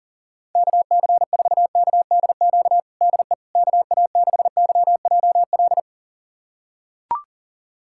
Morse code, heard as a keyed tone, about 40 wpm, 700 Hz. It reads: KC4KDQ DE KA6YJL
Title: USA Callsign DE Callsign - 6 Characters - 40wpm